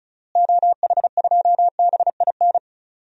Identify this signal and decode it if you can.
Morse code, heard as a keyed tone, about 36 wpm, 700 Hz. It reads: OH2BIN
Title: DXCC-100 Call Signs - 6 Characters - Repeated 3x - 35wpm